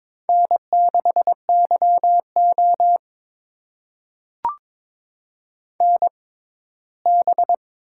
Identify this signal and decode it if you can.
Morse code, heard as a keyed tone, about 21 wpm, spaced as for 15 wpm, 700 Hz. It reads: N6YO NB
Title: USA Call Signs - 4 Characters - 6x Character Spacing - 22wpm